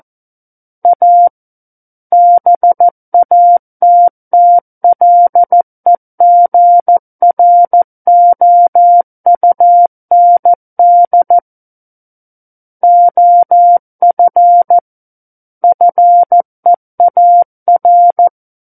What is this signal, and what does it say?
Morse code, heard as a keyed tone, about 14 wpm, 700 Hz. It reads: A BATTLEGROUND OF FEAR
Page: War of the Worlds - 05-Chapter 5 - 14 WPM